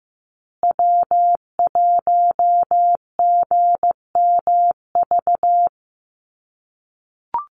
Morse code, heard as a keyed tone, about 15 wpm, 700 Hz. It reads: W1GMV
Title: USA Call Signs Repeated 3x 15wpm